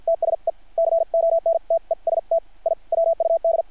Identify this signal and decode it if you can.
Morse code, heard as a keyed tone, about 50 wpm, spaced as for 43 wpm, 660 Hz. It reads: TSE CONTEST I WUD
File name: archived / CRSnet